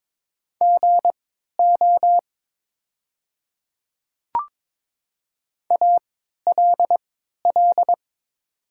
Morse code, heard as a keyed tone, about 22 wpm, spaced as for 13 wpm, 700 Hz. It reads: GO ALL
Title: 3x Character Spacing - Top 100 Words - 22wpm